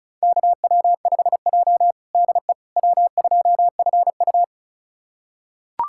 Morse code, heard as a keyed tone, about 36 wpm, 700 Hz. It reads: KW5J DE W2FU
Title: USA Callsign DE Callsign - 4 Characters - 35wpm